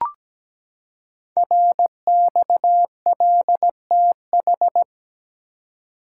Morse code, heard as a keyed tone, about 17 wpm, 700 Hz. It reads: RXLTH
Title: Code Groups - 5 Letters - 17wpm